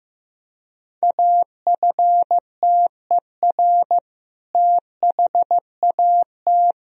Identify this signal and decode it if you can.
Morse code, heard as a keyed tone, about 15 wpm, 700 Hz. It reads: AFTER THAT
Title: Sentences from Top 1000 Words - Review Part 2 - 15wpm